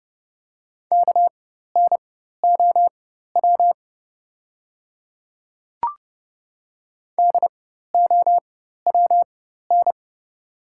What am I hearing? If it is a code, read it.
Morse code, heard as a keyed tone, about 30 wpm, spaced as for 14 wpm, 690 Hz. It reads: KNOW DOWN
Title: 4x Character Spacing - Top 200 Words - 30wpm